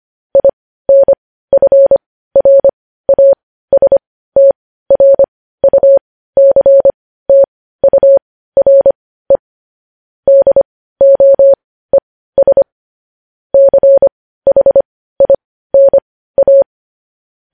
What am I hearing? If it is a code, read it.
Morse code, heard as a keyed tone, about 25 wpm, spaced as for 15 wpm, 550 Hz. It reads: INFRASTRUCTURE DOES CHINA
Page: news headlines in morse code at 15 wpm